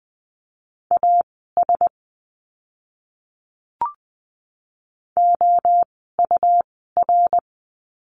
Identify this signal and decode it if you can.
Morse code, heard as a keyed tone, about 20 wpm, spaced as for 15 wpm, 700 Hz. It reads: AS OUR
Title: 160 Lesson 26f - Letter g - Words 20wpm